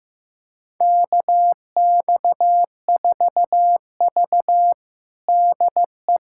Morse code, heard as a keyed tone, about 15 wpm, 700 Hz. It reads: KX4V DE
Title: USA Callsign DE Callsign - 4 Characters - 15wpm